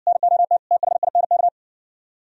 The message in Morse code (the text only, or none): NOT THINK